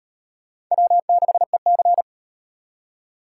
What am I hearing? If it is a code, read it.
Morse code, heard as a keyed tone, about 38 wpm, 700 Hz. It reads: W6EC